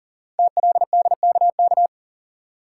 Morse code, heard as a keyed tone, about 42 wpm, 700 Hz. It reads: TLDKK
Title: Code Groups - 5 Letters - 40wpm